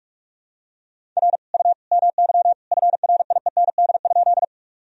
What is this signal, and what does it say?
Morse code, heard as a keyed tone, about 41 wpm, 700 Hz. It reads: R U MY FRIEND?
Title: Sentences from Top 300 Words and Abbreviations 45wpm